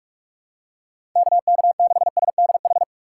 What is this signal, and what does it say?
Morse code, heard as a keyed tone, about 47 wpm, 700 Hz. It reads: KK6SDH
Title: USA Call Signs - 6 Characters - 2x Character Spacing - 45wpm